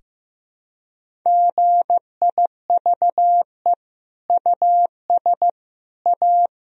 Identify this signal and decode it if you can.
Morse code, heard as a keyed tone, about 15 wpm, 700 Hz. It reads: GIVE US A
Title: Sentences from Top 1000 Words - Review Part 4 - 15wpm